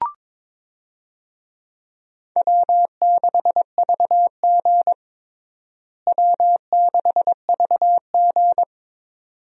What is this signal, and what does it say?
Morse code, heard as a keyed tone, about 22 wpm, 700 Hz. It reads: W6VG W6VG
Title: USA Call Signs - 4 Characters - Repeated 2x - 22wpm